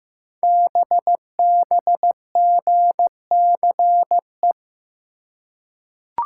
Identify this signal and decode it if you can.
Morse code, heard as a keyed tone, about 15 wpm, 700 Hz. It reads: BBGCE
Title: Code Groups - 5 Letter:Numbers - 15wpm